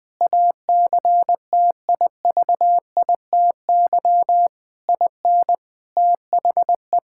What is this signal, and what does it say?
Morse code, heard as a keyed tone, about 20 wpm, 700 Hz. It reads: ACTIVITY IN THE
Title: Sentences from Top 1000 Words - Review Part 6 - 20wpm